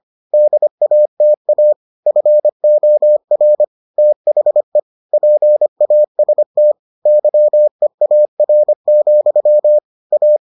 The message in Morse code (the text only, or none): DATA FOR THE PAST YEAR, A